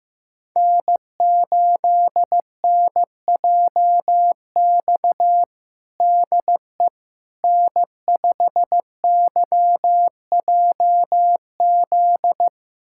Morse code, heard as a keyed tone, about 16 wpm, 700 Hz. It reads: N8NJX DE N5YJZ
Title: USA Callsign DE Callsign - 5 Characters - 15wpm